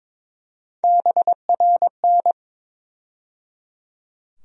Morse code, heard as a keyed tone, about 22 wpm, 700 Hz. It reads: BRN
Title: USA Call Sign Postfixes - 3 Letters 22wpm